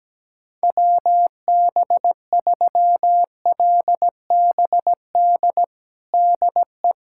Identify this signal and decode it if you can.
Morse code, heard as a keyed tone, about 17 wpm, 700 Hz. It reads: WB3LBD DE
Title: USA Callsign DE Callsign 17wpm